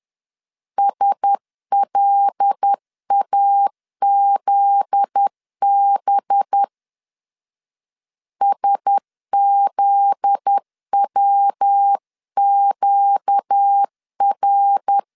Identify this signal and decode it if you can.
Morse code, heard as a keyed tone, about 10 wpm, 800 Hz. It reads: SLAZB SZWQR
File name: LEZIONI / 7 GF9Z / 7 GF9Z 10wpm 2611